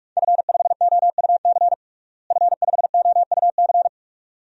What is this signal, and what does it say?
Morse code, heard as a keyed tone, about 45 wpm, 700 Hz. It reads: F5OUC F5OUC
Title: DXCC-100 Call Signs - 5 Characters - Repeated 2x - 45wpm